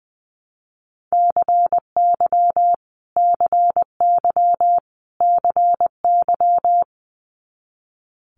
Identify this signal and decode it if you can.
Morse code, heard as a keyed tone, about 21 wpm, 700 Hz. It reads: CY CY CY
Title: Multi-letter Phonemes Repeated 3x 20wpm